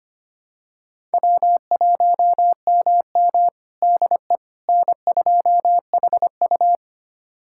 Morse code, heard as a keyed tone, about 25 wpm, 700 Hz. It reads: W1MM DE N2HU
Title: USA Callsign DE Callsign - 4 Characters - 25wpm